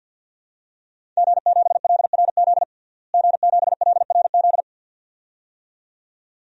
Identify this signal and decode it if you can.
Morse code, heard as a keyed tone, about 49 wpm, 700 Hz. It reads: G7LRZ G7LRZ
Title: DXCC-100 Call Signs - 5 Characters - Repeated 2x - 50wpm